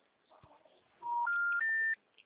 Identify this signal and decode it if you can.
Special information tone.